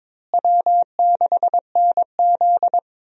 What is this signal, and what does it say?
Morse code, heard as a keyed tone, about 22 wpm, 700 Hz. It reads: W6NZ
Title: USA Call Signs - 4 Characters - Repeated 2x - 22wpm